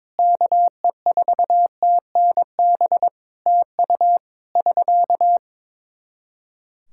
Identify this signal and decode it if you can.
Morse code, heard as a keyed tone, about 22 wpm, 700 Hz. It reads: KE4TNB TU <SK>
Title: Practice QSO - Exchange 4 - Ending - 22wpm